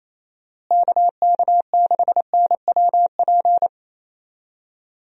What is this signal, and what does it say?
Morse code, heard as a keyed tone, about 28 wpm, 700 Hz. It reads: KK6NWP
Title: USA Call Signs - 6 Characters - Repeated 2x - 28wpm